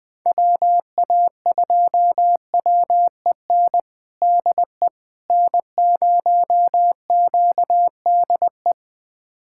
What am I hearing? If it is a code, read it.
Morse code, heard as a keyed tone, about 21 wpm, 690 Hz. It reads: WA2WEN DE N0QDE
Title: USA Callsign DE Callsign 20wpm